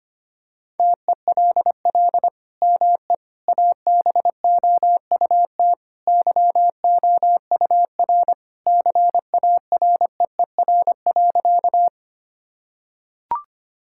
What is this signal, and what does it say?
Morse code, heard as a keyed tone, about 25 wpm, 700 Hz. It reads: TELL ME ABOUT YOUR CAREER.
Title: Sentences from Top 1000 Words - Review Part 6 - 25wpm